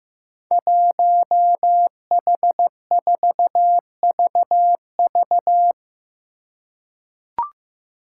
Morse code, heard as a keyed tone, about 15 wpm, 700 Hz. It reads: 1H4VV
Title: Five Challenging Letter-Number Combinations 15wpm